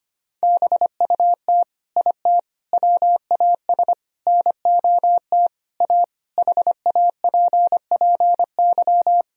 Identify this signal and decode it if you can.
Morse code, heard as a keyed tone, about 25 wpm, 700 Hz. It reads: BUT IT WAS NOT A HAPPY
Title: Sentences from Top 1000 Words - Review Part 1 - 25wpm